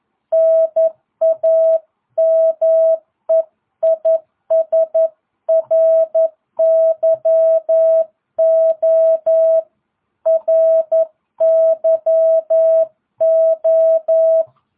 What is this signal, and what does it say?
Morse code, heard as a keyed tone, about 11 wpm, 650 Hz. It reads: NAME IS RYO RYO